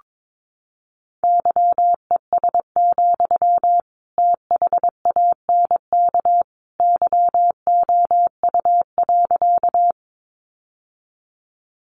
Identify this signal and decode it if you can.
Morse code, heard as a keyed tone, about 22 wpm, 700 Hz. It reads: YES, THANK YOU.
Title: Sentences from Top 600 Words 22wpm